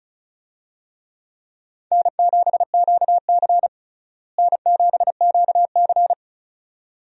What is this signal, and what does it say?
Morse code, heard as a keyed tone, about 36 wpm, 700 Hz. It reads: N7QC N7QC
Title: USA Call Signs - 4 Characters - Repeated 2x - 35wpm